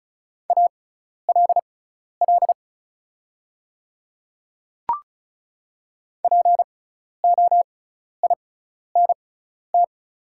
Morse code, heard as a keyed tone, about 36 wpm, spaced as for 12 wpm, 700 Hz. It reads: ALL POINT
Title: 6x Character Spacing - Top 300 Words - 35wpm